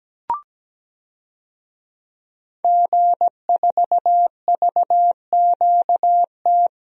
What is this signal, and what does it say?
Morse code, heard as a keyed tone, about 17 wpm, 700 Hz. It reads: G4VQT
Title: DXCC-100 Call Signs - 5 Characters - 17wpm